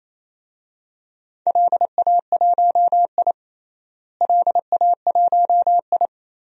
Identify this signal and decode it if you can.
Morse code, heard as a keyed tone, about 27 wpm, 700 Hz. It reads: LA1I LA1I
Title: DXCC-100 Call Signs - 4 Characters - Repeated 3x - 28wpm